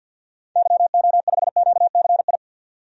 Morse code, heard as a keyed tone, about 50 wpm, 700 Hz. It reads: YO5QCI
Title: DXCC-100 Call Signs - 6 Characters - 50wpm